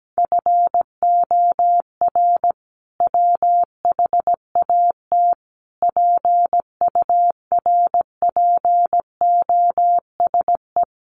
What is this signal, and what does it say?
Morse code, heard as a keyed tone, about 17 wpm, 700 Hz. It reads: FOR WHAT PURPOSE
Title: Sentences from Top 800 Words 17wpm